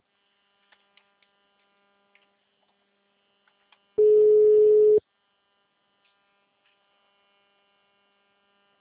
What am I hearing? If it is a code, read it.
Ringback tone (Europe, 425 Hz).
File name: out- -2006- - - .122